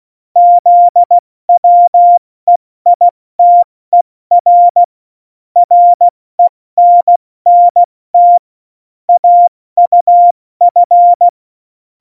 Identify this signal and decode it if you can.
Morse code, heard as a keyed tone, about 16 wpm, 700 Hz. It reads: ZWEITER RENNT AUF
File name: Morsekurs / Witze / Witz 17. November 2020, 70 F80